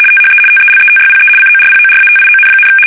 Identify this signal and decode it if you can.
RTTY, about 75 baud. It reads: THE TELETYPE CORPORATION,
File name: fsk850-75B / teletype-corp